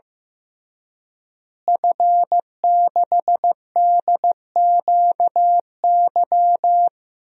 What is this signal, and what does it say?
Morse code, heard as a keyed tone, about 15 wpm, 700 Hz. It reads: F6DQY
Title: DXCC-100 Call Signs - 5 Characters - Repeated 3x - 15wpm